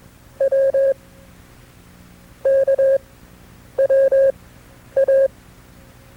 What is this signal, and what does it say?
Morse code, heard as a keyed tone, about 21 wpm, spaced as for 7 wpm, 540 Hz. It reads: WKWA
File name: FreeFiles / CW-practice / KB6MT-tape2-side2